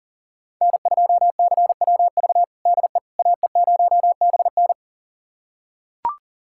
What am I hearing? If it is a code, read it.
Morse code, heard as a keyed tone, about 40 wpm, 700 Hz. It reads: N2CWV DE AE0BN
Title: USA Callsign DE Callsign - 5 Characters - 40wpm